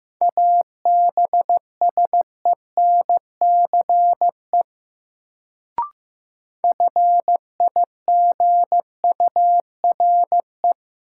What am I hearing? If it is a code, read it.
Morse code, heard as a keyed tone, about 15 wpm, 700 Hz. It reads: ABSENCE FIGURE
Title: Sets of 4 Words - New Word Drill - 15wpm